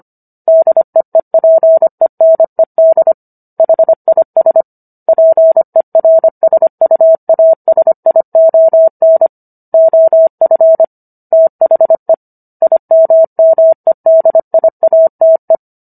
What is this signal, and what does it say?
Morse code, heard as a keyed tone, about 25 wpm, 650 Hz. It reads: DEEPENED HIS PERSUASION OF THE IMMEDIATE